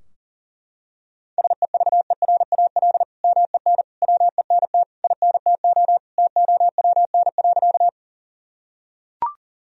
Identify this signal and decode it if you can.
Morse code, heard as a keyed tone, about 40 wpm, 700 Hz. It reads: SEVERAL MEN WENT INTO TOWN.
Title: Sentences from Top 1000 Words - Review Part 2 - 40wpm